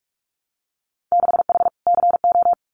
Morse code, heard as a keyed tone, about 46 wpm, 700 Hz. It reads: 6H CO
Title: Winter Field Day Exchanges Encore 45wpm